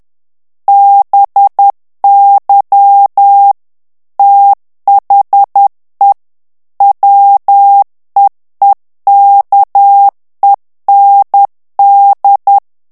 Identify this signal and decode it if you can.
Morse code, heard as a keyed tone, about 10 wpm, 800 Hz. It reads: BY THE WEEKEND
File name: morse mp3 / ANGLAIS texte clair 2 vitesse 10 mots minute